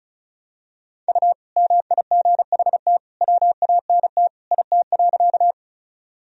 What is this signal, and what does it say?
Morse code, heard as a keyed tone, about 36 wpm, 700 Hz. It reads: U MIGHT WANT IT.
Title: Sentences from Top 300 Words and Abbreviations 35wpm